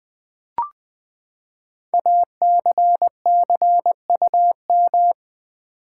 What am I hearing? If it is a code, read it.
Morse code, heard as a keyed tone, about 20 wpm, 700 Hz. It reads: ACCUM
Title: Build A Word - 5 Letters At A Time - 12-Letter Words - 20wpm